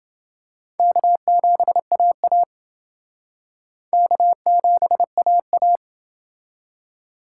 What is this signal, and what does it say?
Morse code, heard as a keyed tone, about 28 wpm, 700 Hz. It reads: K7AA K7AA